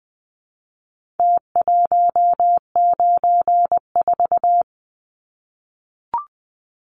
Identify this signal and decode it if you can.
Morse code, heard as a keyed tone, about 20 wpm, 700 Hz. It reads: T194
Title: Numbers - Contest Style - 4-Digit 20wpm